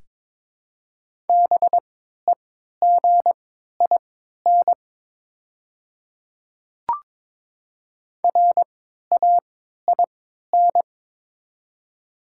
Morse code, heard as a keyed tone, about 22 wpm, spaced as for 13 wpm, 700 Hz. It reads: BEGIN RAIN
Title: 3x Character Spacing - Top 300 Words - 22wpm